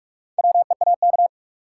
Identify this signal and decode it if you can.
Morse code, heard as a keyed tone, about 45 wpm, 700 Hz. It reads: WEAK
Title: Four Letter Words - 45wpm